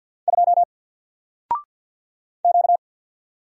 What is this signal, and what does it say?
Morse code, heard as a keyed tone, about 50 wpm, 700 Hz. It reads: 2 Q
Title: Single Letter-Number 50wpm